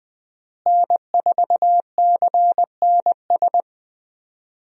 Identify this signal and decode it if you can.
Morse code, heard as a keyed tone, about 20 wpm, 700 Hz. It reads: N4CNS
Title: USA Call Signs - 5 Characters - Repeated 2x - 20wpm